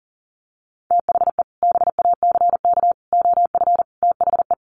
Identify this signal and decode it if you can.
Morse code, heard as a keyed tone, about 40 wpm, 700 Hz. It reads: THE BACK OF THE
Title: Sets of 4 Words 40wpm